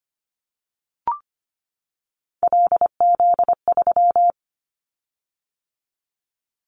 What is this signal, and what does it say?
Morse code, heard as a keyed tone, about 25 wpm, 700 Hz. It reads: LZ3
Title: Callsign Prefixes for Top 50 DXCC - 25wpm